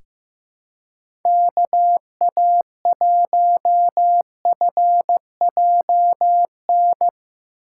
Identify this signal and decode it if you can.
Morse code, heard as a keyed tone, about 15 wpm, 700 Hz. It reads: KA1FJN
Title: USA Call Signs - 6 Characters - 3x Character Spacing - 15wpm